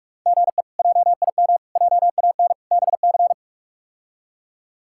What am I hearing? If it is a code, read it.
Morse code, heard as a keyed tone, about 46 wpm, 700 Hz. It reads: GE JIM JAN BC
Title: K1USN SST QSO Practice - 45wpm